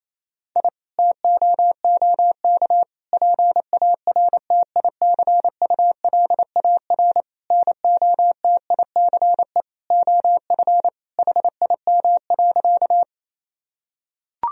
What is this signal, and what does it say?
Morse code, heard as a keyed tone, about 28 wpm, 700 Hz. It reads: I TOOK PARTICULAR NOTICE OF HIM.
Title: Sentences from Top 1000 Words - Review Part 4 - 28wpm